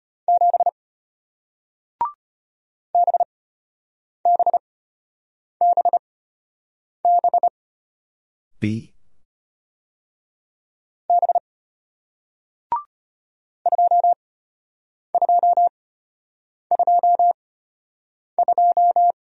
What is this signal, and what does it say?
Morse code, heard as a keyed tone, about 33 wpm, spaced as for 8 wpm, 700 Hz. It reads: 7 BBBB B 2222